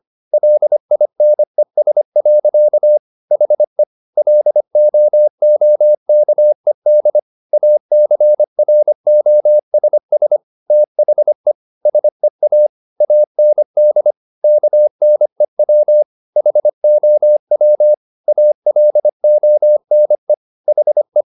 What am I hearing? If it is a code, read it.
Morse code, heard as a keyed tone, about 25 wpm, 600 Hz. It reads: LINES. HE LOOKED ACROSS THE SEA AND KNEW HOW ALONE HE
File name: the old man and the sea part 1 25wpm